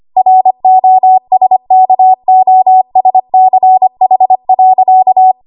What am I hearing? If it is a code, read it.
Morse code, heard as a keyed tone, about 25 wpm, 750 Hz. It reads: ROSKOSCH.
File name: Archive / 25 WPM / 25WPM